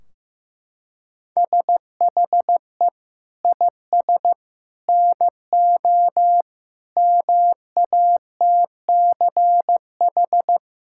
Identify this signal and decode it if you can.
Morse code, heard as a keyed tone, about 15 wpm, 700 Hz. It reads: SHE IS NO MATCH
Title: Sentences from Top 1600 Words 15wpm